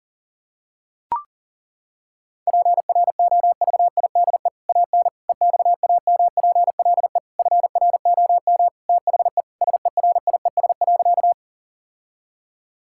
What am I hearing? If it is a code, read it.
Morse code, heard as a keyed tone, about 37 wpm, 700 Hz. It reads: PROVIDE AN EXAMPLE FROM THE SERIES.
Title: Sentences from Top 1000 Words - Review Part 5 - 40wpm